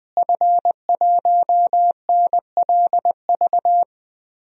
Morse code, heard as a keyed tone, about 20 wpm, 700 Hz. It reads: F1NLV
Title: DXCC-100 Call Signs - 5 Characters - Repeated 3x - 20wpm